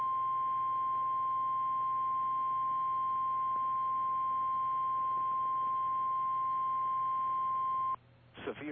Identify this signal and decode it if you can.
Emergency Alert System (SAME) tones (the 1050 Hz attention tone), with speech. It is heard from 0.6 to 7.7 s.
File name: eas / KKPRKXPN / KKPRKXPN